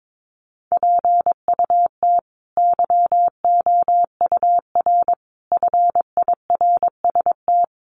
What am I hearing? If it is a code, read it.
Morse code, heard as a keyed tone, about 22 wpm, 700 Hz. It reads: PUT YOUR FIRST
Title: A Sound Of Thunder 22wpm